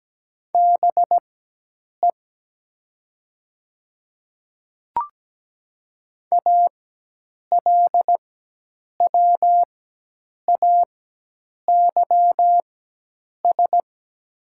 Morse code, heard as a keyed tone, about 17 wpm, spaced as for 8 wpm, 700 Hz. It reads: BE ALWAYS
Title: 4x Character Spacing - Top 300 Words - 17wpm